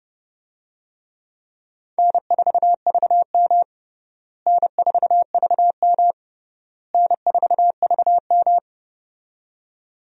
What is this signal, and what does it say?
Morse code, heard as a keyed tone, about 30 wpm, 700 Hz. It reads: N4VM N4VM N4VM
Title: USA Call Signs - 4 Characters - Repeated 3x - 30wpm